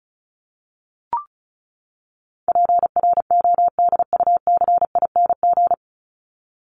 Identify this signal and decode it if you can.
Morse code, heard as a keyed tone, about 34 wpm, 700 Hz. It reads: PRODUCING
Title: ING Suffix - Top 100 Words with Base Words 35wpm